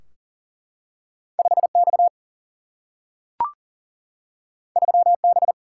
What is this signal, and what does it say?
Morse code, heard as a keyed tone, about 40 wpm, 700 Hz. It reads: 5X 3B
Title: Single-Letter Single-Number 40wpm